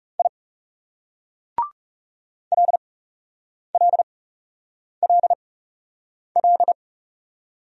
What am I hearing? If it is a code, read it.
Morse code, heard as a keyed tone, about 40 wpm, spaced as for 8 wpm, 700 Hz. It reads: I LLLL